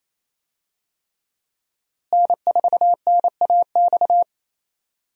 Morse code, heard as a keyed tone, about 28 wpm, 700 Hz. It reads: N4NAX
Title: USA Call Signs - 5 Characters - Repeated 3x - 28wpm